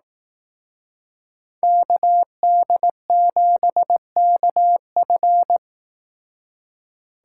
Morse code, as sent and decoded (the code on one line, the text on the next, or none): -.- -.. --... -.- ..-.
KD7KF